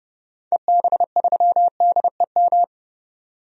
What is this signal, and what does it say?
Morse code, heard as a keyed tone, about 30 wpm, 700 Hz. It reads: EB3DEM
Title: DXCC-100 Call Signs - 6 Characters - 30wpm